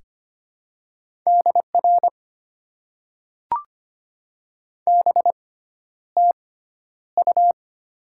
Morse code, heard as a keyed tone, about 25 wpm, spaced as for 11 wpm, 690 Hz. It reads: DR BTU
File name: Top 90 QSO Elements - 6x Character Spacing - 25wpm